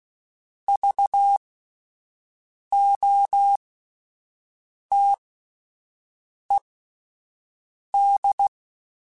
Morse code, heard as a keyed tone, about 16 wpm, spaced as for 5 wpm, 800 Hz. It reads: VOTED